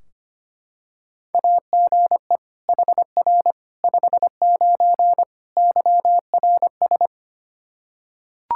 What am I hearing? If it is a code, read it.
Morse code, heard as a keyed tone, about 25 wpm, 700 Hz. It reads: AGE HR 59 YRS
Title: Practice QSO - Exchange 3 - More Info - 25wpm